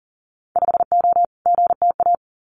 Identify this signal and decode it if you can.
Morse code, heard as a keyed tone, about 38 wpm, 700 Hz. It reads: 5O GTA
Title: Winter Field Day Exchanges Encore 40wpm